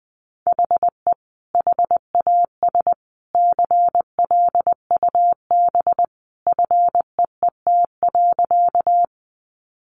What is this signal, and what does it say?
Morse code, heard as a keyed tone, about 20 wpm, 700 Hz. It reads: HE HAS CLUB FEET.
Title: Sentences from Top 1200 Words 20wpm